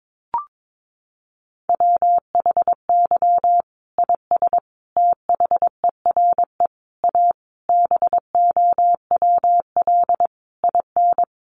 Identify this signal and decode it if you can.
Morse code, heard as a keyed tone, about 22 wpm, 700 Hz. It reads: WHY IS THERE A BOWL IN
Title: Sentences from Top 1200 Words 22wpm